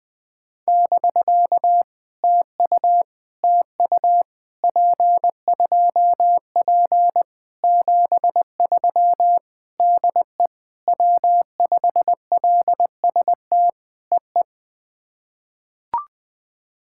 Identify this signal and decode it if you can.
Morse code, heard as a keyed tone, about 20 wpm, 700 Hz. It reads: <BK> TU TU P2P 73 DE W5LST EE
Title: POTA QSO Practice - 20wpm